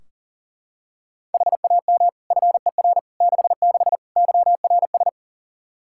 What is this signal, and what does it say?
Morse code, heard as a keyed tone, about 40 wpm, 700 Hz. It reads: HAM FER 66 YRS
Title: Practice QSO - Exchange 3 - More Info - 40wpm